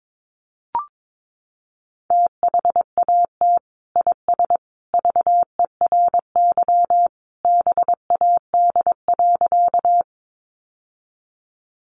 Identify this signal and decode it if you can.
Morse code, heard as a keyed tone, about 22 wpm, 700 Hz. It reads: THAT IS VERY BAD.
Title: Sentences from Top 700 Words 22wpm